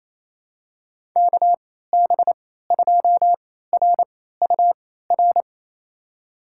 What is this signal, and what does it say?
Morse code, heard as a keyed tone, about 28 wpm, spaced as for 16 wpm, 700 Hz. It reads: KB2RUR
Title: USA Call Signs - 6 Characters - 3x Character Spacing - 28wpm